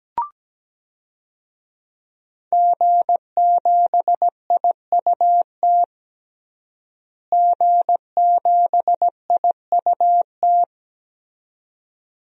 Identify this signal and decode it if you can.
Morse code, heard as a keyed tone, about 17 wpm, 700 Hz. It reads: G7IUT G7IUT
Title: DXCC-100 Call Signs - 5 Characters - Repeated 2x - 17wpm